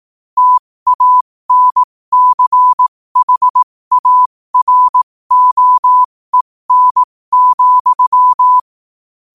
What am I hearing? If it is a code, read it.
Morse code, heard as a keyed tone, about 18 wpm, spaced as for 15 wpm, 1000 Hz. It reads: TANCHAROEN,
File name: qotdfiles / 15wpm